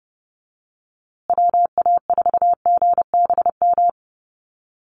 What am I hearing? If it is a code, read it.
Morse code, heard as a keyed tone, about 30 wpm, 700 Hz. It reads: WA4GBM
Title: USA Call Signs Repeated 3x 30wpm